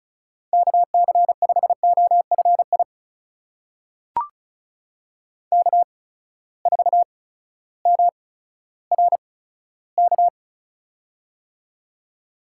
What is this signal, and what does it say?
Morse code, heard as a keyed tone, about 36 wpm, spaced as for 16 wpm, 700 Hz. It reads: KC5OFI K4MRK
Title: USA Call Signs - 8x Character Spacing - 35wpm